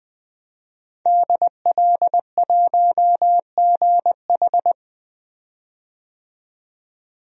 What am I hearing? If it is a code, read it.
Morse code, heard as a keyed tone, about 20 wpm, 700 Hz. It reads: DL1GH
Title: DXCC-100 Call Signs - 5 Characters - 20wpm